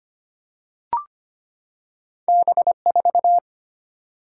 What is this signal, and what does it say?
Morse code, heard as a keyed tone, about 25 wpm, 690 Hz. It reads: B4
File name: Two Letter-Number Combinations - 25wpm